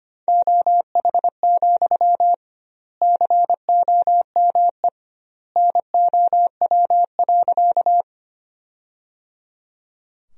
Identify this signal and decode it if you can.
Morse code, heard as a keyed tone, about 25 wpm, 700 Hz. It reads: OH, COME NOW.
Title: 2x Word-Spacing - Sentences from Top 500 Words - 25wpm